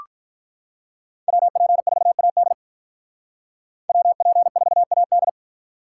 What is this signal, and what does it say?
Morse code, heard as a keyed tone, about 45 wpm, 700 Hz. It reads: WP4AD WP4AD